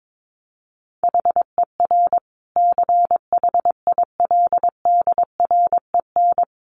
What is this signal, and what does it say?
Morse code, heard as a keyed tone, about 22 wpm, 700 Hz. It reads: HER CHILDREN
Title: Sentences from Top 900 Words 22wpm